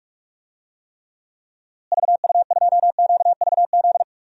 Morse code, heard as a keyed tone, about 46 wpm, 700 Hz. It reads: VU2QVZ
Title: DXCC-100 Call Signs - 6 Characters - Repeated 3x - 45wpm